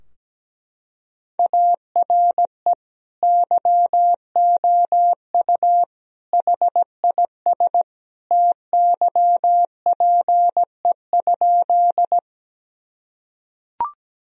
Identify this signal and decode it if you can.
Morse code, heard as a keyed tone, about 17 wpm, 700 Hz. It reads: ARE YOU HIS TYPE?
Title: Sentences from Top 1000 Words - Review Part 3 - 17wpm